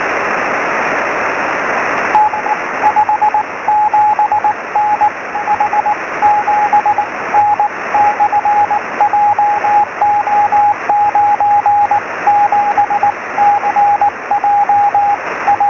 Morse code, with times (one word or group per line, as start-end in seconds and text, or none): none